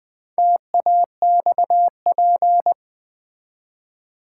Morse code, heard as a keyed tone, about 20 wpm, 700 Hz. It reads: TAXP
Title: Build A Word - 2 Letters At A Time - 8-Letter Words - 20wpm